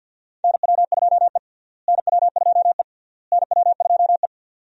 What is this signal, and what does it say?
Morse code, heard as a keyed tone, about 52 wpm, 700 Hz. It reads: NW2E NW2E NW2E
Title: USA Call Signs - 4 Characters - Repeated 3x - 50wpm